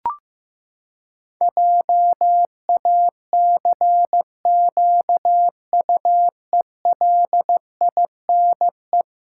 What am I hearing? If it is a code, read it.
Morse code, heard as a keyed tone, about 15 wpm, 700 Hz. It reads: JACQUELINE
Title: Pangram Challenge v2 - 15wpm